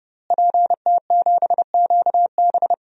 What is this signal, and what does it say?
Morse code, heard as a keyed tone, about 30 wpm, 700 Hz. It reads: PT7QB
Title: DXCC-100 Call Signs - 5 Characters - Repeated 3x - 30wpm